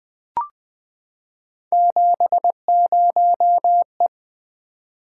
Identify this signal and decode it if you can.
Morse code, heard as a keyed tone, about 20 wpm, 700 Hz. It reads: 70E
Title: Three Letter-Number Combinations - 20wpm